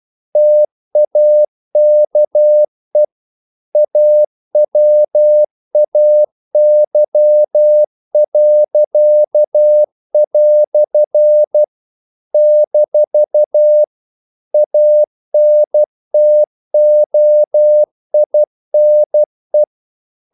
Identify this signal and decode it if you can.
Morse code, heard as a keyed tone, about 12 wpm, 600 Hz. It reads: TAKE AWAY." - ANTOINE
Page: English quotations in CW (telegraphy - morse)